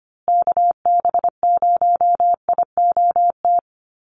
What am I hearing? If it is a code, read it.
Morse code, heard as a keyed tone, about 25 wpm, 700 Hz. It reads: KB0IOT
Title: USA Call Signs Repeated 2x 25wpm